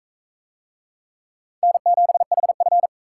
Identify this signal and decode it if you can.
Morse code, heard as a keyed tone, about 43 wpm, 700 Hz. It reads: N7HF